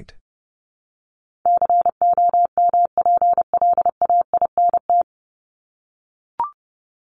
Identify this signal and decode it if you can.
Morse code, heard as a keyed tone, about 30 wpm, 700 Hz. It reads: COMPLAINT
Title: Top 1251-1500 Words 30wpm